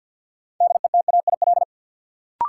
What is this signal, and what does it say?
Morse code, heard as a keyed tone, about 50 wpm, 700 Hz. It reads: DETAIL
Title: Sentences from Top 1500 Words - New Word Drill Alt - 50wpm